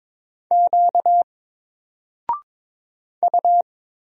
Morse code, heard as a keyed tone, about 22 wpm, 700 Hz. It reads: Q U
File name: Single-Letter-Number-22wpm